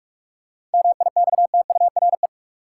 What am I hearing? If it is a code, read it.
Morse code, heard as a keyed tone, about 46 wpm, 700 Hz. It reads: MIXTURE
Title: Seven Letter Words - 45wpm